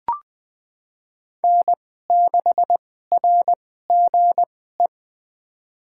Morse code, heard as a keyed tone, about 20 wpm, spaced as for 14 wpm, 700 Hz. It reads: N6RGE
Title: USA Call Signs - 5 Characters - 2x Character Spacing - 20wpm